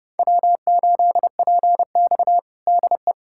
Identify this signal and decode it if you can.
Morse code, heard as a keyed tone, about 30 wpm, 700 Hz. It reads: W8PX DE
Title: USA Callsign DE Callsign - 4 Characters - 30wpm